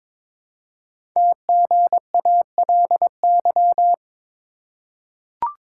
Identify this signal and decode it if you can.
Morse code, heard as a keyed tone, about 22 wpm, 700 Hz. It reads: TGALY
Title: Code Groups - 5 Letters - 22wpm